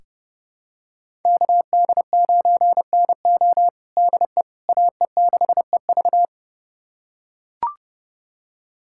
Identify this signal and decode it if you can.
Morse code, heard as a keyed tone, about 31 wpm, 700 Hz. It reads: KD9NO DE AE6EV
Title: USA Callsign DE Callsign - 5 Characters - 30wpm